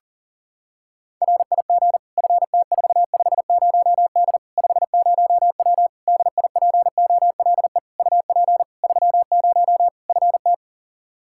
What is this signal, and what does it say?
Morse code, heard as a keyed tone, about 40 wpm, 700 Hz. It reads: RIG FT450D 50W DIPOLE UP 30 FT
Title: Practice QSO - Exchange 2 - Info - 40wpm